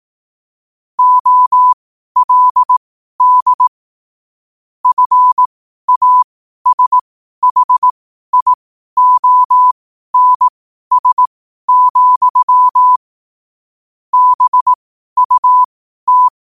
Morse code, heard as a keyed tone, about 18 wpm, spaced as for 12 wpm, 1000 Hz. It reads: OLD FASHIONS, BUT
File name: qotdfiles / 12wpm